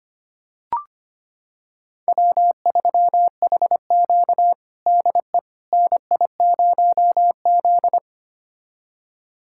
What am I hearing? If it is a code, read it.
Morse code, heard as a keyed tone, about 25 wpm, 700 Hz. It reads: W3HQ DE NI0Z
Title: USA Callsign DE Callsign - 4 Characters - 25wpm